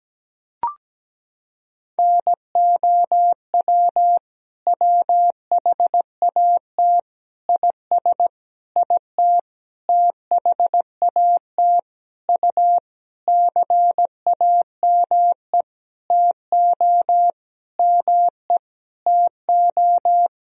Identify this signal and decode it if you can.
Morse code, heard as a keyed tone, about 17 wpm, 700 Hz. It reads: NOW WHAT IS IT THAT U CAME TO ME TO
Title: Sentences from Top 200 Words and Abbreviations 17wpm